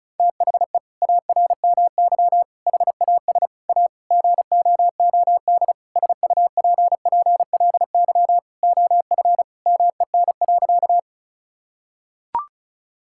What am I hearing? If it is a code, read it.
Morse code, heard as a keyed tone, about 36 wpm, 700 Hz. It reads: THE ARMY HAS A GOOD SUPPLY OF MEN.
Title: Sentences from Top 1000 Words - Review Part 5 - 35wpm